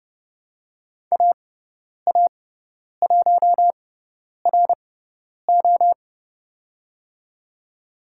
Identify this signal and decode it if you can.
Morse code, heard as a keyed tone, about 30 wpm, spaced as for 10 wpm, 690 Hz. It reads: AA1RO
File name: Lesson 18g - Letter m - Call Signs 30wpm